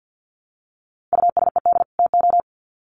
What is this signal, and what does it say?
Morse code, heard as a keyed tone, about 50 wpm, 700 Hz. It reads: USED TO
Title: Sets of 2 Words 50wpm